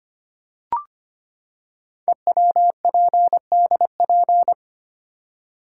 Morse code, heard as a keyed tone, about 25 wpm, 700 Hz. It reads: EWPDP
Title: Code Groups - 5 Letters - 25wpm